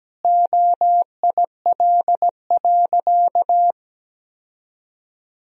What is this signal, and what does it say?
Morse code, heard as a keyed tone, about 17 wpm, 700 Hz. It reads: OIL.
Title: Sentences from Top 1000 Words - Review Part 4 - 17wpm